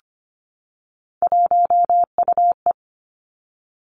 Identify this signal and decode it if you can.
Morse code, heard as a keyed tone, about 25 wpm, 700 Hz. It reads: 1UE
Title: Three Challenging Letter-Number Combinations 25wpm